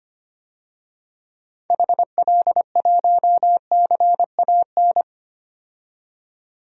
Morse code, heard as a keyed tone, about 25 wpm, 700 Hz. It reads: HL1CAN
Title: DXCC-100 Call Signs - 6 Characters - 25wpm